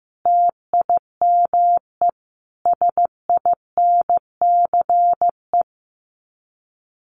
Morse code, heard as a keyed tone, about 15 wpm, 700 Hz. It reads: TIME SINCE
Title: Sets of 4 Words 15wpm